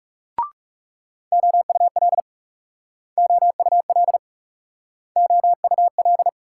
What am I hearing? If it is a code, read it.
Morse code, heard as a keyed tone, about 38 wpm, 700 Hz. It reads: OUL OUL OUL